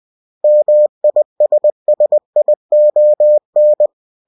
Morse code, heard as a keyed tone, about 20 wpm, 600 Hz. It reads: MISSION
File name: cw-books / winnie-the-pooh / winnie-20wpm-page-0113